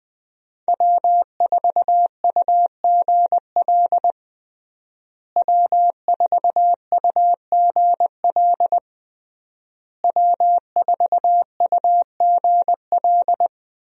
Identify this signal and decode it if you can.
Morse code, heard as a keyed tone, about 20 wpm, 700 Hz. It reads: W4UGL W4UGL W4UGL
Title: USA Call Signs - 5 Characters - Repeated 3x - 20wpm